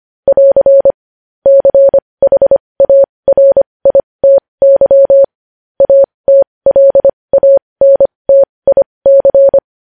Morse code, heard as a keyed tone, about 25 wpm, spaced as for 20 wpm, 550 Hz. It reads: <AR> CHARITY ATLANTIC